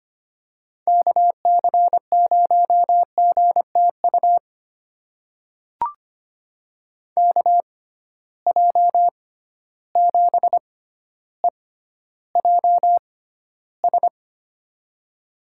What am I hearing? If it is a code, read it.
Morse code, heard as a keyed tone, about 25 wpm, spaced as for 13 wpm, 700 Hz. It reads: KC0GTU KJ7EJS
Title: USA Call Signs - 6 Characters - 6x Character Spacing - 25wpm